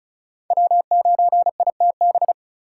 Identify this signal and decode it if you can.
Morse code, heard as a keyed tone, about 36 wpm, 700 Hz. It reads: W9ITB
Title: USA Call Signs - 5 Characters - 2x Character Spacing - 35wpm